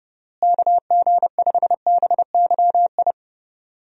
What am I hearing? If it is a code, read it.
Morse code, heard as a keyed tone, about 30 wpm, 700 Hz. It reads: KG5BYI
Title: USA Call Signs - 6 Characters - Repeated 3x - 30wpm